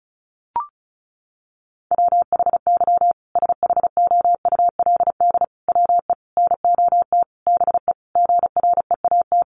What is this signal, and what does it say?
Morse code, heard as a keyed tone, about 34 wpm, 700 Hz. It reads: WHY SHOULD WE NOT BE GREAT
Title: Sentences from Top 1500 Words 35wpm